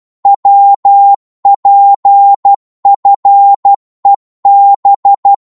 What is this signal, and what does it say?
Morse code, heard as a keyed tone, about 12 wpm, 800 Hz. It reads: WPFEB
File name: Morse / ltr-12wpm